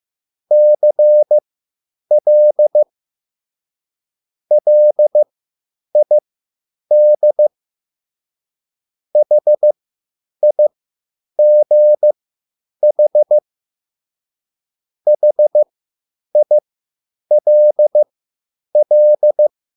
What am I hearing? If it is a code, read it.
Morse code, heard as a keyed tone, about 15 wpm, spaced as for 9 wpm, 600 Hz. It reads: CL LID HIGH HILL
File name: QSL / CW lesson